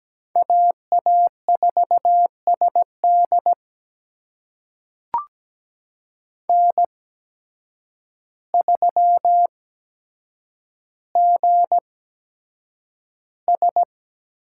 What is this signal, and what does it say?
Morse code, heard as a keyed tone, about 17 wpm, spaced as for 8 wpm, 700 Hz. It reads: AA4SD N3GS
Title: USA Call Signs - 5 Characters - 8x Character Spacing - 17wpm